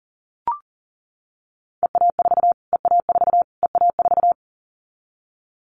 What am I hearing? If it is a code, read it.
Morse code, heard as a keyed tone, about 37 wpm, 700 Hz. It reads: EA4 EA4 EA4
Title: Callsign Prefixes for Top 25 DXCC - Repeated 3x - 40wpm